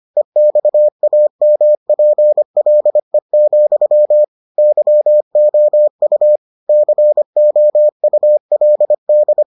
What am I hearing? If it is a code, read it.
Morse code, heard as a keyed tone, about 25 wpm, 600 Hz. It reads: EXAMPLE, YOU COULD